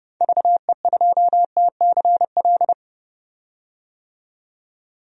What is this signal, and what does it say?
Morse code, heard as a keyed tone, about 30 wpm, 700 Hz. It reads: VE2TCL
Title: DXCC-100 Call Signs - 6 Characters - 30wpm